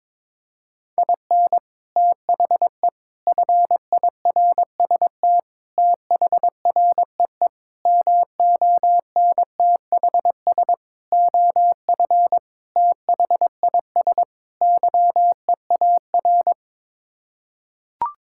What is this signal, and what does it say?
Morse code, heard as a keyed tone, about 21 wpm, 700 Hz. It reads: IN THE FIRST THREE MONTHS OF THIS YEAR
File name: practice / 22wpm / Sets of 8 Words - 22wpm